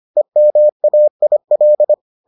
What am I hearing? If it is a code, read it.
Morse code, heard as a keyed tone, about 25 wpm, 600 Hz. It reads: EMAIL